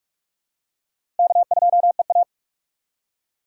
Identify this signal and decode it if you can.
Morse code, heard as a keyed tone, about 41 wpm, 700 Hz. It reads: K2EA
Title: USA Call Signs - 4x Character Spacing - 45wpm